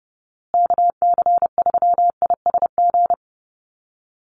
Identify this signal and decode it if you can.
Morse code, heard as a keyed tone, about 30 wpm, 700 Hz. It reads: KC3ISG
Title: USA Call Signs Repeated 3x 30wpm